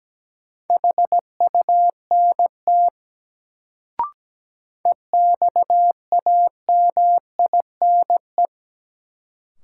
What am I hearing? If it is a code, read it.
Morse code, heard as a keyed tone, about 17 wpm, 700 Hz. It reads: HUNT EXAMINE
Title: Sentences from Top 1600 Words - New Word Drill Alt - 17wpm